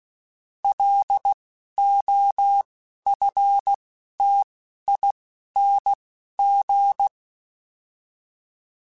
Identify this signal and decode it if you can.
Morse code, heard as a keyed tone, about 16 wpm, 800 Hz. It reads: L O F T I N G